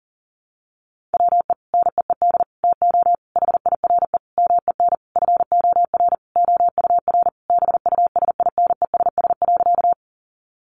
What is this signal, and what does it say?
Morse code, heard as a keyed tone, about 40 wpm, 700 Hz. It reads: WE NEED TO HIRE MEN FOR OUR BUSINESS.
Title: Sentences from Top 1400 Words 40wpm